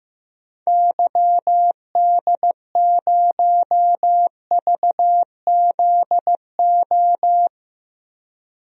Morse code, heard as a keyed tone, about 15 wpm, 700 Hz. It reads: YD0VZO
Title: DXCC-100 Call Signs - 6 Characters - 15wpm